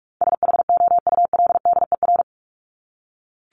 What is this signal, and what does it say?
Morse code, heard as a keyed tone, about 44 wpm, 700 Hz. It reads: SHOULDER
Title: Top 1501-1750 Words 45wpm